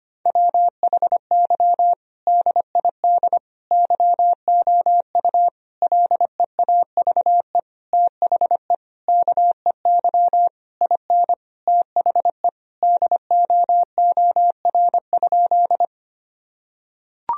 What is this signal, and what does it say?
Morse code, heard as a keyed tone, about 25 wpm, 700 Hz. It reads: WHY DID YOU LEAVE THE KEY IN THE DOOR?
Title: Sentences from Top 1000 Words - Review Part 4 - 25wpm